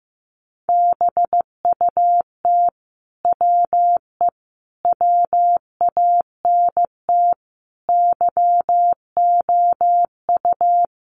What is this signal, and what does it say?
Morse code, heard as a keyed tone, about 15 wpm, 700 Hz. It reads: BUT WE WANT YOU
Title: Sentences from Top 700 Words 15wpm